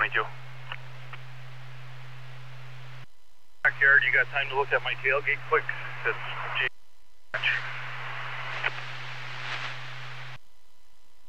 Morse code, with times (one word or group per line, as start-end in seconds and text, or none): none